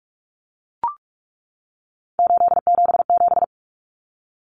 Morse code, heard as a keyed tone, about 44 wpm, 700 Hz. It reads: 877
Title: Common Cut Numbers - 3-Digit 45wpm